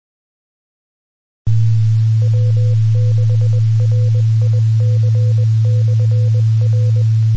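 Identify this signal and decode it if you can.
Morse code, heard as a keyed tone, about 20 wpm, 500 Hz, with noise. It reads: W6RIC/R
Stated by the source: Signal on the 2 m band at 146.850 MHz.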